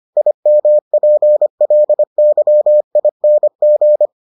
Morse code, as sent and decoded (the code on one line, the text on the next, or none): .. -- .--. .-.. -.-- .. -. --.
IMPLYING